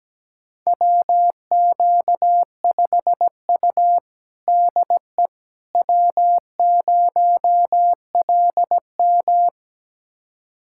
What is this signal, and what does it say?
Morse code, heard as a keyed tone, about 17 wpm, 700 Hz. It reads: WQ5U DE W0LM
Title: USA Callsign DE Callsign - 4 Characters - 17wpm